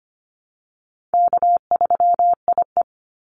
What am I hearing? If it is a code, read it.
Morse code, heard as a keyed tone, about 25 wpm, 700 Hz. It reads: K3IE
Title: Easy Call Signs Repeated 3x - 25wpm